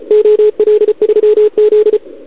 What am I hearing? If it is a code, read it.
Morse code, heard as a keyed tone, about 36 wpm, 440 Hz, with radio noise. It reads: OL3Z
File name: CQWWCW2021 / OL3Z